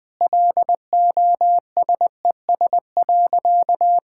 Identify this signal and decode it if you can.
Morse code, heard as a keyed tone, about 20 wpm, 700 Hz. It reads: LOSES.
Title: Sentences with 6-Word Phrases - 20wpm